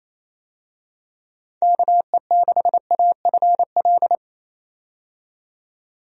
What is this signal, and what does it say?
Morse code, heard as a keyed tone, about 28 wpm, 700 Hz. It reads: KE6AFL
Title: USA Call Signs - 6 Characters - 28wpm